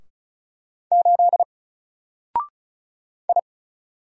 Morse code, heard as a keyed tone, about 35 wpm, 700 Hz. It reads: 8 I
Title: Single Character (Letter:Number:Punctuation:Prosign) 35wpm